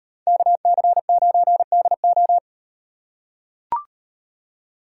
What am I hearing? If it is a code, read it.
Morse code, heard as a keyed tone, about 38 wpm, 700 Hz. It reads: KC9DO